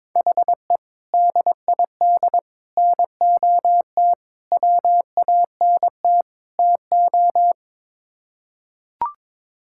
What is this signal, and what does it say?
Morse code, heard as a keyed tone, about 22 wpm, 700 Hz. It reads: HE DID NOT WANT TO
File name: practice / 22wpm / Build A Phrase - 5-Word Phrases - 22wpm